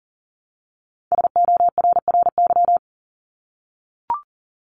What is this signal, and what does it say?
Morse code, heard as a keyed tone, about 40 wpm, 700 Hz. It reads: SORRY
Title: Top 1751-2000 Words 40wpm